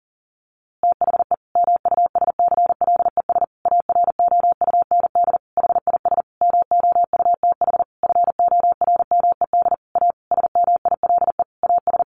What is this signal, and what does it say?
Morse code, heard as a keyed tone, about 37 wpm, 700 Hz. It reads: THE MUSCLES AROUND HIS MOUTH FORMED A SMILE AS
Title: A Sound Of Thunder 40wpm